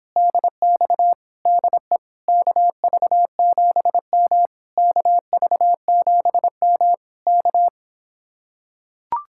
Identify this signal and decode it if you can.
Morse code, heard as a keyed tone, about 26 wpm, 700 Hz. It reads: DX DE KV7M KV7M K